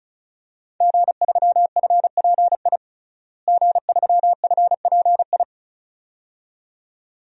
Morse code, heard as a keyed tone, about 36 wpm, 700 Hz. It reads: G3FPI G3FPI
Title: DXCC-100 Call Signs - 5 Characters - Repeated 2x - 35wpm